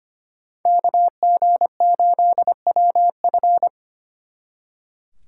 Morse code, heard as a keyed tone, about 29 wpm, 700 Hz. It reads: KG8WF
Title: Speed Racing - USA 2x2 Call Signs 25wpm